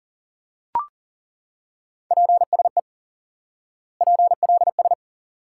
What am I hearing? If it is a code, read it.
Morse code, heard as a keyed tone, about 40 wpm, 700 Hz. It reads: PSE PLS
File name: IQE Course 3 - Lesson 89 - YAGI - Two Elements - 40wpm